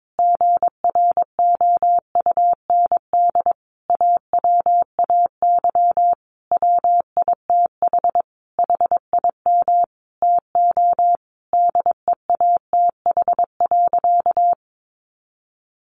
Morse code, heard as a keyed tone, about 22 wpm, 700 Hz. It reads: GROUND AWAY WITH HIM TO DEATH.
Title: Sentences from Top 1100 Words 22wpm